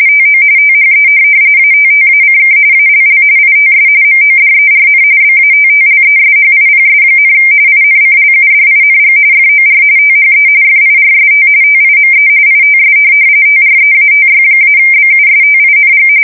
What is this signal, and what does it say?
RTTY, about 45.45 baud. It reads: TO THE ATTACK ALL ITS MODERN BATTLESHIPS, INCLUDING FOR THE FIRST TIME ITS TREATY-BUSTING
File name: fsk170 / USS-New-Jersey